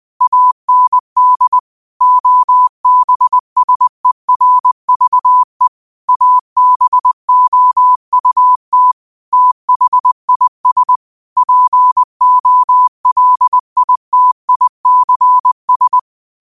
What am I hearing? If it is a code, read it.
Morse code, heard as a keyed tone, about 20 wpm, 1000 Hz. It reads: AND OBSERVE ABOUT THIS POLITICS
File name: qotdfiles / 20wpm